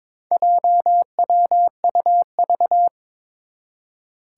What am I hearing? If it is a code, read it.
Morse code, heard as a keyed tone, about 22 wpm, 700 Hz. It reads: JWUV
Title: Four Challenging Letter-Number Combinations 22wpm